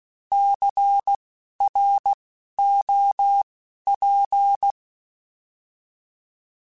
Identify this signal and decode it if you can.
Morse code, heard as a keyed tone, about 16 wpm, spaced as for 12 wpm, 800 Hz. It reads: CROP
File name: Alphabetical / Alphabetical mp3s / Cs @10